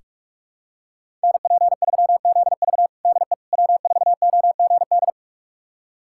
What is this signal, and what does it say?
Morse code, heard as a keyed tone, about 47 wpm, 700 Hz. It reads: NP3ZV DE W4OGD
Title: USA Callsign DE Callsign - 5 Characters - 45wpm